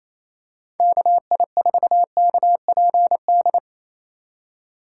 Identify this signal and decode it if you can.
Morse code, heard as a keyed tone, about 28 wpm, 700 Hz. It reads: KI4KPD
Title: USA Call Signs - 6 Characters - 4x Character Spacing - 28wpm